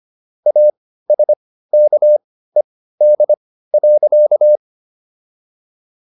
Morse code, heard as a keyed tone, about 25 wpm, spaced as for 15 wpm, 600 Hz. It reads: ASKED.